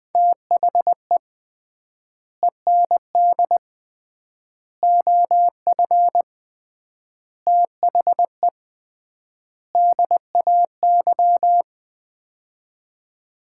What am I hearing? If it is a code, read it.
Morse code, heard as a keyed tone, about 20 wpm, 700 Hz. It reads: THE END OF THE DAY
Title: 3x Word-Spacing - Sets of 5 Words - 20wpm